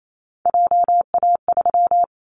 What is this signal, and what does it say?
Morse code, heard as a keyed tone, about 28 wpm, 700 Hz. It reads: JA3
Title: Callsign Prefixes for Top 25 DXCC - 28wpm